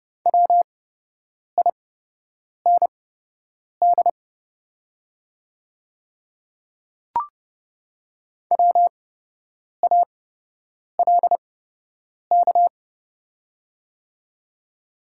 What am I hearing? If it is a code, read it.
Morse code, heard as a keyed tone, about 30 wpm, spaced as for 8 wpm, 700 Hz. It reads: WIND WALK
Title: 8x Character Spacing - Top 500 Words - 30wpm